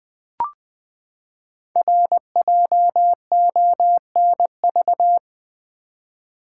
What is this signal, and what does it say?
Morse code, heard as a keyed tone, about 20 wpm, 700 Hz. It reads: RJONV
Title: Code Groups - 5 Letters - 20wpm